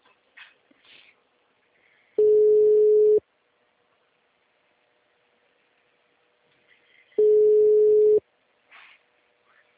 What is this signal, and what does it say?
Ringback tone (Europe, 425 Hz).